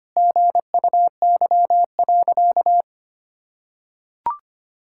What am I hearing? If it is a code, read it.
Morse code, heard as a keyed tone, about 25 wpm, 700 Hz. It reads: GUY.
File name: Sentences with 3-Word Phrases - 25wpm